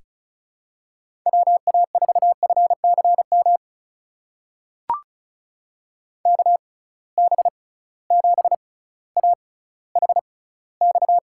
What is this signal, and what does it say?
Morse code, heard as a keyed tone, about 36 wpm, spaced as for 18 wpm, 700 Hz. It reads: WA4FCM KB7AHX
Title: USA Call Signs - 6x Character Spacing - 35wpm